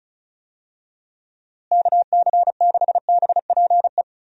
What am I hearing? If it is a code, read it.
Morse code, heard as a keyed tone, about 36 wpm, 700 Hz. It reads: KC6BPE
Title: USA Call Signs - 6 Characters - Repeated 2x - 35wpm